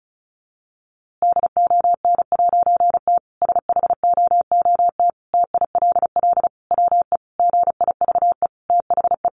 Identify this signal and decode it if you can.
Morse code, heard as a keyed tone, about 34 wpm, 700 Hz. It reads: DON'T SHOOT TILL WE GIVE THE
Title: A Sound Of Thunder 35wpm